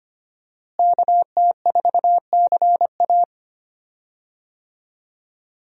Morse code, heard as a keyed tone, about 25 wpm, 700 Hz. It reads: KT4CA
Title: USA Call Signs - 5 Characters - Repeated 2x - 25wpm